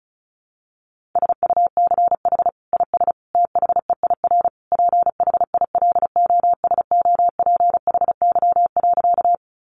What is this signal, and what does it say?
Morse code, heard as a keyed tone, about 34 wpm, 700 Hz. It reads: SUCH IS THEIR PHILOSOPHY.
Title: Sentences from Top 900 Words 35wpm